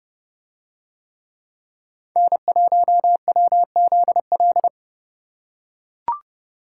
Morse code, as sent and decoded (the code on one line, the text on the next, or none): -. .---- .-- --.. .-..
N1WZL